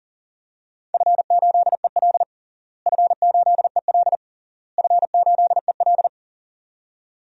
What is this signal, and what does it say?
Morse code, heard as a keyed tone, about 40 wpm, 700 Hz. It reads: F8EL F8EL F8EL
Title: DXCC-100 Call Signs - 4 Characters - Repeated 3x - 40wpm